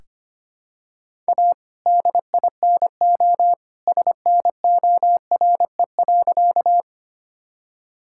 Morse code, heard as a keyed tone, about 25 wpm, 700 Hz. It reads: A DINO SNORE.
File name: practice / 25wpm / All-Age Appropriate Jokes - 25wpm